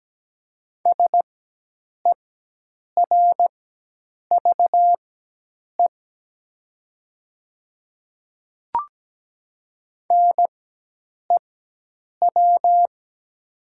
Morse code, heard as a keyed tone, about 17 wpm, spaced as for 8 wpm, 700 Hz. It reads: SERVE NEW
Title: 4x Character Spacing - Top 500 Words - 17wpm